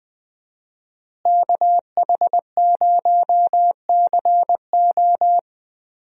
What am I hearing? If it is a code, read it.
Morse code, heard as a keyed tone, about 20 wpm, 700 Hz. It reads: KH0CO
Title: USA Call Signs - 5 Characters - Repeated 2x - 20wpm